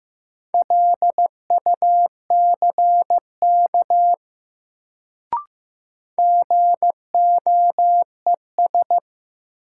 Morse code, heard as a keyed tone, about 15 wpm, 700 Hz. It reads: LUCK GOES
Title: Sentences from Top 1500 Words - New Word Drill Alt - 15wpm